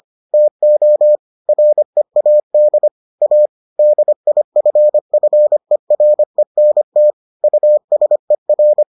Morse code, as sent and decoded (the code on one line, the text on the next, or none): - --- / .-. .- -.. / .- / -.. .. ..-. ..-. .-. -. - / ..- ... .-.
TO READ A DIFFERENT USER